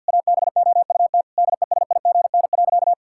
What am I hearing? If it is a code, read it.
Morse code, heard as a keyed tone, about 48 wpm, 700 Hz. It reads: ABOUT DESIGN.
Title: Sentences from Top 1000 Words - Review Part 5 - 50wpm